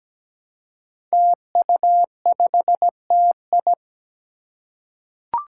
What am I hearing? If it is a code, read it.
Morse code, heard as a keyed tone, about 17 wpm, 700 Hz. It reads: TU5TI
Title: Five Challenging Letter-Number Combinations 17wpm